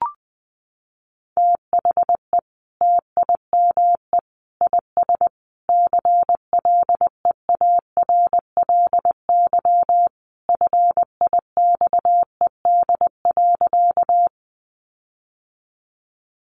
Morse code, heard as a keyed tone, about 20 wpm, 700 Hz. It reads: THE TIME IS CLEARLY FIXED.
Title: Sentences from Top 1300 Words 20wpm